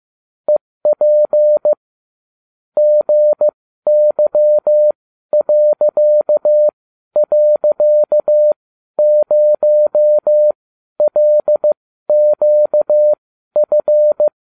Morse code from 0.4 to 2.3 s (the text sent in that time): EP